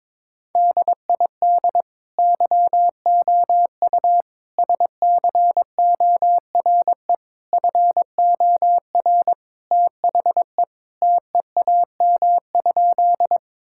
Morse code, heard as a keyed tone, about 22 wpm, 700 Hz. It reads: DID YOU SCORE FOR THE TEAM?
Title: Sentences from Top 1600 Words 22wpm